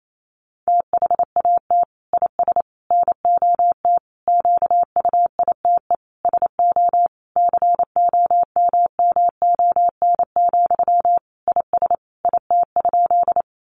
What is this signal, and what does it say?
Morse code, heard as a keyed tone, about 28 wpm, 700 Hz. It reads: THAT IS NOT QUITE SO COMMON, IS IT?
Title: Sentences from Top 700 Words 28wpm